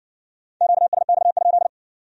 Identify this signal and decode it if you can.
Morse code, heard as a keyed tone, about 66 wpm, 700 Hz. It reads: QSX?